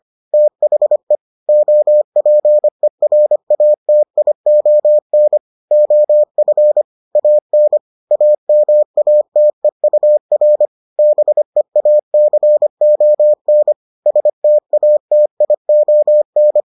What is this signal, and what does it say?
Morse code, heard as a keyed tone, about 25 wpm, 600 Hz. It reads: THE OPERATION OF AN AMATEUR BEACON STATION